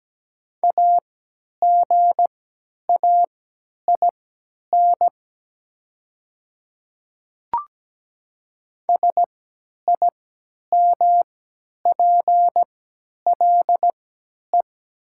Morse code, heard as a keyed tone, about 17 wpm, spaced as for 10 wpm, 690 Hz. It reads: AGAIN SIMPLE
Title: 3x Character Spacing - Top 500 Words - 17wpm